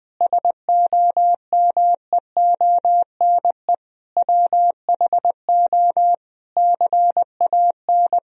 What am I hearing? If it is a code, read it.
Morse code, heard as a keyed tone, about 20 wpm, 700 Hz. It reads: SOMEONE WHO CAN
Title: Sentences from Top 1000 Words - Review Part 4 - 20wpm